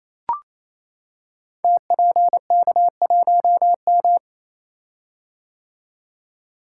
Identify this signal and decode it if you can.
Morse code, heard as a keyed tone, about 28 wpm, 700 Hz. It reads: TPK1M
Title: Code Groups - 5 Letter:Numbers - 28wpm